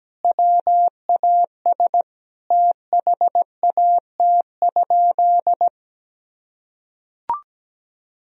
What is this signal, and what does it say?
Morse code, heard as a keyed tone, about 17 wpm, 700 Hz. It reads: WAS THAT?
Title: Sentences from Top 100 Words and Abbreviations 17wpm